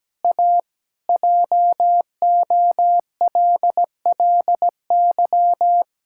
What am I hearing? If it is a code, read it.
Morse code, heard as a keyed tone, about 17 wpm, 700 Hz. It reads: A JOLLY
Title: Pangram Challenge v2 - 17wpm